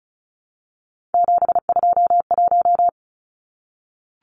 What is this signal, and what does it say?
Morse code, heard as a keyed tone, about 34 wpm, 700 Hz. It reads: 721
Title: Numbers - 3-Digit 35wpm